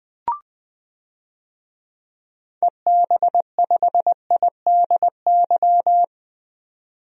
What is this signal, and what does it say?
Morse code, heard as a keyed tone, about 20 wpm, 700 Hz. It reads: EB5IDY
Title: DXCC-100 Call Signs - 6 Characters - 20wpm